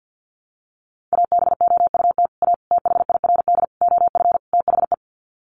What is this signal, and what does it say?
Morse code, heard as a keyed tone, about 48 wpm, 700 Hz. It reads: ABOUT A THIRD OF THE
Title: Sets of 5 Words Encore2 50wpm